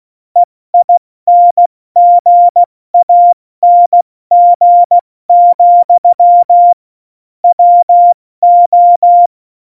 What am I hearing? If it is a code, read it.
Morse code, heard as a keyed tone, about 16 wpm, 700 Hz. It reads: EINGANG, WO
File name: Morsekurs / Witze / Witz 13. November 2018, 70 F80